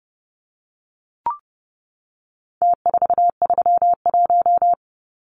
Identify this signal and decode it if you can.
Morse code, heard as a keyed tone, about 29 wpm, 700 Hz. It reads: T431
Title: Numbers - Contest Style - 4-Digit 30wpm